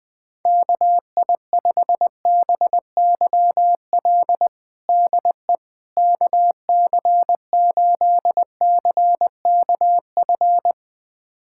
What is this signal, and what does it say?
Morse code, heard as a keyed tone, about 20 wpm, 700 Hz. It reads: KI5BYL DE KC8CKF
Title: USA Callsign DE Callsign - 6 Characters - 20wpm